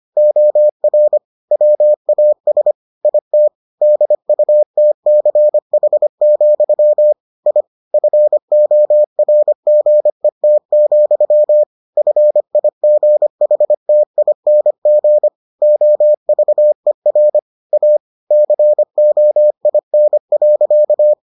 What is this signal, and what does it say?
Morse code, heard as a keyed tone, about 25 wpm, 600 Hz. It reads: OR WAS IT DUTCH, I FORGET, FIGHTING OVER A COIN.